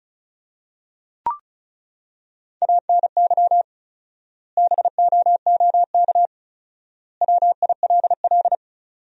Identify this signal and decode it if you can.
Morse code, heard as a keyed tone, about 36 wpm, 700 Hz. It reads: ANY BOOK WILL
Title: 4x Word-Spacing - Sentences from Top 300 Words - 35wpm